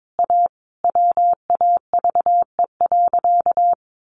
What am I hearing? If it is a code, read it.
Morse code, heard as a keyed tone, about 22 wpm, 700 Hz. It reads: A WAVE.
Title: Sentences from Top 1400 Words 22wpm